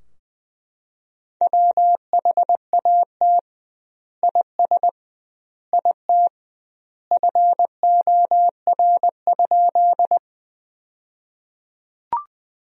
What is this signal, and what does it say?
Morse code, heard as a keyed tone, about 21 wpm, 700 Hz. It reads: WHAT IS IT FOR?
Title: 2x Word-Spacing - Sentences from Top 100 Words - 20wpm